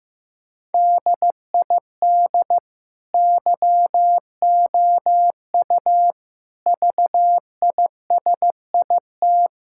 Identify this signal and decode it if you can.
Morse code, heard as a keyed tone, about 15 wpm, 700 Hz. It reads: DID YOU VISIT
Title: Sentences from Top 1000 Words - Review Part 6 - 15wpm